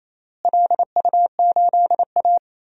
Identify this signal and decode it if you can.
Morse code, heard as a keyed tone, about 28 wpm, 700 Hz. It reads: LU8A
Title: DXCC-100 Call Signs - 4 Characters - Repeated 3x - 28wpm